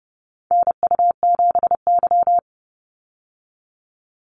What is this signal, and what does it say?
Morse code, heard as a keyed tone, about 30 wpm, 700 Hz. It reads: NU7Y
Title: Call Signs Repeated 2x - 30wpm